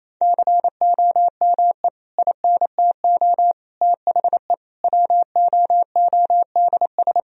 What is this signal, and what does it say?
Morse code, heard as a keyed tone, about 28 wpm, 700 Hz. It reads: COME INTO THE WOODS
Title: Sentences from Top 1000 Words - Review Part 2 - 28wpm